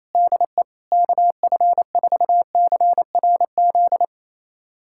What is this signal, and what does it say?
Morse code, heard as a keyed tone, about 28 wpm, 700 Hz. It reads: DE KF4CRZ
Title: USA Callsign DE Callsign - 6 Characters - 28wpm